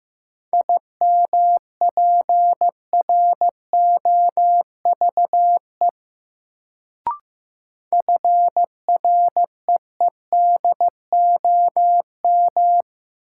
Morse code, heard as a keyed tone, about 15 wpm, 700 Hz. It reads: IMPROVE FREEDOM
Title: Sentences from Top 800 Words - New Word Drill - 15wpm